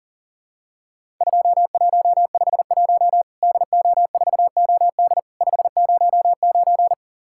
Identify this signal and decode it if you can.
Morse code, heard as a keyed tone, about 40 wpm, 700 Hz. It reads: 2151 DO4OD 509
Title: Worked All Europe DX Contest - QTC Exchange - 40wpm